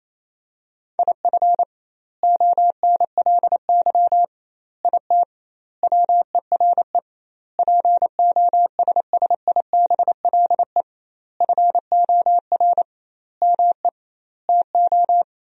Morse code, heard as a keyed tone, about 28 wpm, 700 Hz. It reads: IF ONLY IT WERE POSSIBLE FOR ME TO
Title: 2x Word-Spacing - Sentences from Top 500 Words - 28wpm